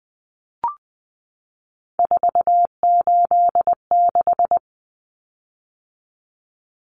Morse code, heard as a keyed tone, about 20 wpm, 700 Hz. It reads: 486
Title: Common Cut Numbers - 3-Digit 20wpm